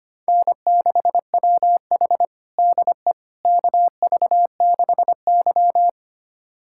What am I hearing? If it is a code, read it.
Morse code, heard as a keyed tone, about 25 wpm, 700 Hz. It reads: N6WH DE KV6Y
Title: USA Callsign DE Callsign - 4 Characters - 25wpm